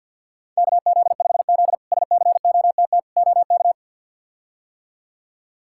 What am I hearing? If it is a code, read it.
Morse code, heard as a keyed tone, about 52 wpm, 700 Hz. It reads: KZ5Z SCOTT OK
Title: K1USN SST QSO Practice - 50wpm